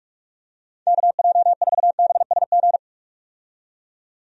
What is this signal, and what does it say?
Morse code, heard as a keyed tone, about 46 wpm, 700 Hz. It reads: KJ4BSG
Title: USA Call Signs - 6 Characters - 45wpm